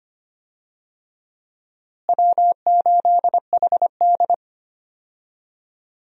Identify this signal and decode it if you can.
Morse code, heard as a keyed tone, about 25 wpm, 700 Hz. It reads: W8HD
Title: USA Call Signs - 4 Characters - Repeated 3x - 25wpm